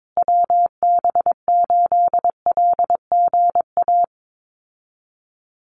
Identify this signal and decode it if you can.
Morse code, heard as a keyed tone, about 22 wpm, 700 Hz. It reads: WB8LGA
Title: USA Call Signs Repeated 2x 22wpm